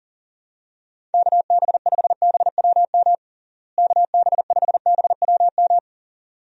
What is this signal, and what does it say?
Morse code, heard as a keyed tone, about 40 wpm, 700 Hz. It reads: KB5BWM KB5BWM
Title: USA Call Signs - 6 Characters - Repeated 2x - 40wpm